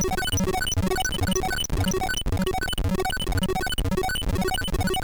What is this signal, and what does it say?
There is no code.